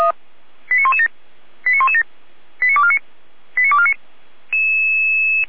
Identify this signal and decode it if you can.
Five-tone selective call (ZVEI), heard as quick runs of five tones, tones of about 70 ms. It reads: 89117 89117 89249 89249
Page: SDR/ZVEI